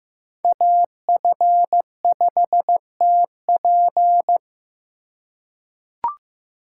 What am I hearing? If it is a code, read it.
Morse code, heard as a keyed tone, about 15 wpm, 700 Hz. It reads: AF5TP
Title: USA Call Signs - 5 Characters - 2x Character Spacing - 15wpm